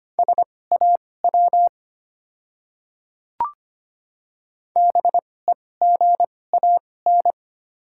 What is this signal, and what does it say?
Morse code, heard as a keyed tone, about 25 wpm, spaced as for 18 wpm, 700 Hz. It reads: SAW BEGAN
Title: 2x Character Spacing - Top 300 Words - 25wpm